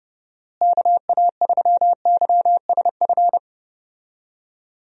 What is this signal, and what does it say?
Morse code, heard as a keyed tone, about 30 wpm, 700 Hz. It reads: KA3YSF
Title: USA Call Signs - 6 Characters - Repeated 3x - 30wpm